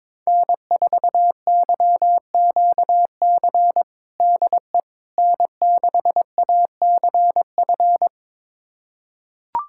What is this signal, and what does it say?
Morse code, heard as a keyed tone, about 22 wpm, 700 Hz. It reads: N4YQC DE N6ACF
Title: USA Callsign DE Callsign - 5 Characters - 22wpm